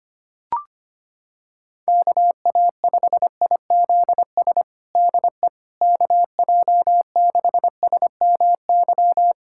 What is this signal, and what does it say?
Morse code, heard as a keyed tone, about 25 wpm, 700 Hz. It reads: KA5IZS DE KJ6SMY
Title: USA Callsign DE Callsign - 6 Characters - 25wpm